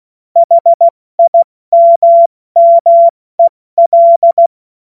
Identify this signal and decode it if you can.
Morse code, heard as a keyed tone, about 16 wpm, 680 Hz. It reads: HIMMEL
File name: Morsekurs / Witze / Witz 09. Dezember 2025 70 F80